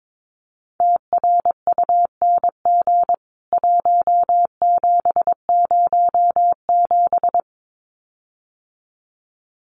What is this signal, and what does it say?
Morse code, heard as a keyed tone, about 22 wpm, 700 Hz. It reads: TRUNG 1707
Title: CWT Exchange 22wpm